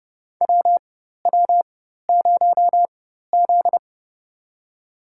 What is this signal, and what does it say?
Morse code, heard as a keyed tone, about 30 wpm, spaced as for 14 wpm, 700 Hz. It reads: WW0Z
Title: USA Call Signs - 4x Character Spacing - 30wpm